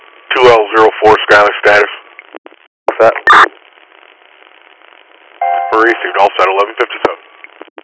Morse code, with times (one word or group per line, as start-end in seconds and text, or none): none